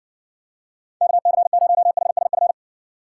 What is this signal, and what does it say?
Morse code, heard as a keyed tone, about 60 wpm, 700 Hz. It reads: KC0HSF